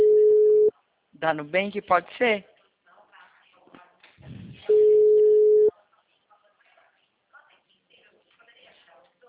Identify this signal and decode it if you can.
Ringback tone (Europe, 425 Hz), with speech.